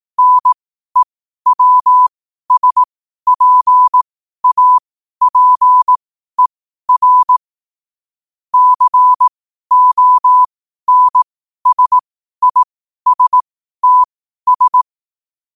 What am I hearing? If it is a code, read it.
Morse code, heard as a keyed tone, about 18 wpm, spaced as for 12 wpm, 1000 Hz. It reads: NEWSPAPER CONSISTS
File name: qotdfiles / 12wpm